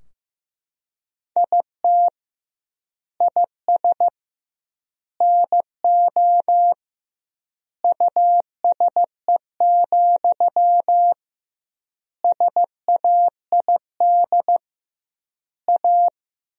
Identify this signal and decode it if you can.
Morse code, heard as a keyed tone, about 15 wpm, 700 Hz. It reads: IT IS NO USE, SAID A
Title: 2x Word-Spacing - Sentences from Top 500 Words - 15wpm